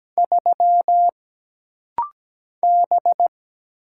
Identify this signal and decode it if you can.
Morse code, heard as a keyed tone, about 17 wpm, 700 Hz. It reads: 3 B
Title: Single Letter-Number Alt - Rapid-Fire 17wpm